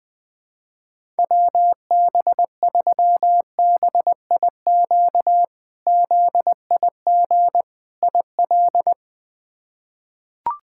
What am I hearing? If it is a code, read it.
Morse code, heard as a keyed tone, about 20 wpm, 700 Hz. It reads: WB3BIQ ZIG IL
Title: K1USN SST QSO Practice - 20wpm